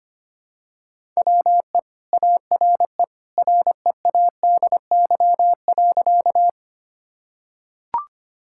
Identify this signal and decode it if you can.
Morse code, heard as a keyed tone, about 25 wpm, 700 Hz. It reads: WE ARE READY.
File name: practice / 25wpm / Sentences with 2-Word Phrases - 25wpm